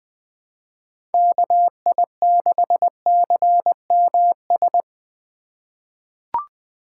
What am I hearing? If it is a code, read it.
Morse code, heard as a keyed tone, about 20 wpm, 700 Hz. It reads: KI6CMS
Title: USA Call Signs - 6 Characters - 3x Character Spacing - 20wpm